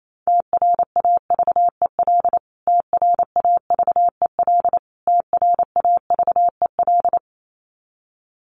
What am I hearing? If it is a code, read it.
Morse code, heard as a keyed tone, about 28 wpm, 700 Hz. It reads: TRAVEL TRAVEL TRAVEL
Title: Top 500 Words Repeated 28wpm